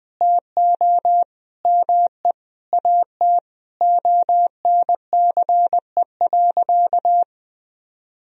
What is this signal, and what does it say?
Morse code, heard as a keyed tone, about 20 wpm, 700 Hz. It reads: TO ME AT ONCE.
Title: Sentences from Top 500 Words and Abbreviations 20wpm